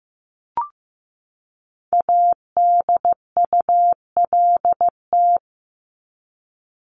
Morse code, heard as a keyed tone, about 15 wpm, 700 Hz. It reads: ADULT
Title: Top 1501-1750 Words 15wpm